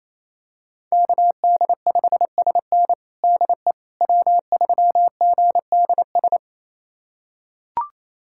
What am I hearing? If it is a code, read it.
Morse code, heard as a keyed tone, about 28 wpm, 700 Hz. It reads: KD5SN DE W3GDS
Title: USA Callsign DE Callsign 28wpm